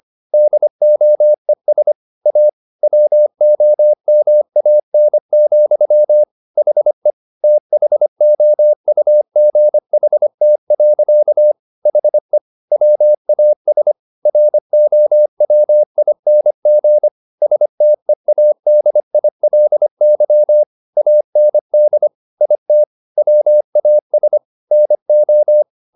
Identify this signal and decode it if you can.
Morse code, heard as a keyed tone, about 25 wpm, 600 Hz. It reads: DOES A WOMAN, HE THOUGHT. HE WAS ROWING STEADILY AND IT WAS NO
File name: the old man and the sea part 1 25wpm